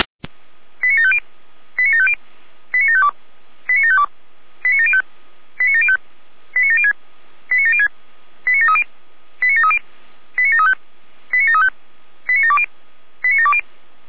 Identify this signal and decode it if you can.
Five-tone selective call (ZVEI), heard as quick runs of five tones, tones of about 70 ms. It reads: 89750 89750 89752 89752 89705 89705 89706 89706 89730 89730 89735 89735 89720 89720
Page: /455/SDR/ZVEI/